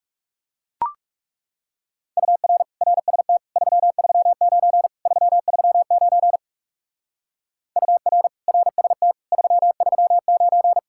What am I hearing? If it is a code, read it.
Morse code, heard as a keyed tone, about 36 wpm, 700 Hz. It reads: UR RST 339 339 UR RST 339
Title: Speed-Racing Common QSO Phrases - 30wpm